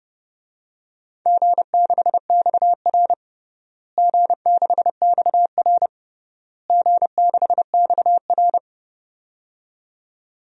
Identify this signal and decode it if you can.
Morse code, heard as a keyed tone, about 30 wpm, 700 Hz. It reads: G6XR G6XR G6XR
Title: DXCC-100 Call Signs - 4 Characters - Repeated 3x - 30wpm